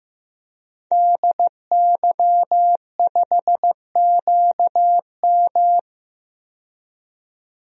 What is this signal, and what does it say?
Morse code, heard as a keyed tone, about 15 wpm, 700 Hz. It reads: DY5QM
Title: DXCC-100 Call Signs - 5 Characters - 15wpm